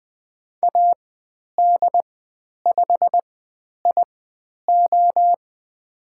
Morse code, heard as a keyed tone, about 20 wpm, spaced as for 10 wpm, 700 Hz. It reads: AD5IO
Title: 095 Lesson 16g - Letter c - Call Signs 20wpm